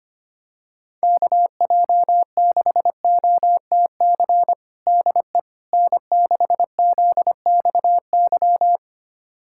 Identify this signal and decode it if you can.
Morse code, heard as a keyed tone, about 25 wpm, 700 Hz. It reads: KJ6OTC DE N6ZXY
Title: USA Callsign DE Callsign 25wpm